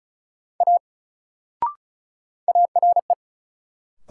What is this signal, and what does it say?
Morse code, heard as a keyed tone, about 36 wpm, 700 Hz. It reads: A ARE
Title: Sentences from Top 100 Words - New Word Drill - 35wpm